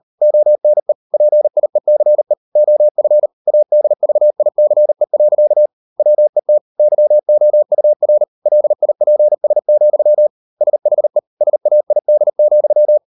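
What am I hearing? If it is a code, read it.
Morse code, heard as a keyed tone, about 38 wpm, 600 Hz. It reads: ONE PIECE OF ADVICE. WET YOUR LIPS, SHE SAID,
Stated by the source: Morse code practice at 38 WPM, sending Ken Follett's novel Never in English, from CW med Gnister.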